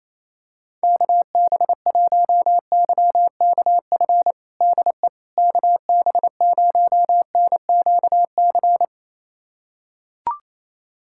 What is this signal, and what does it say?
Morse code, heard as a keyed tone, about 27 wpm, 700 Hz. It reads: KB1YKF DE KB0NQC
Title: USA Callsign DE Callsign - 6 Characters - 28wpm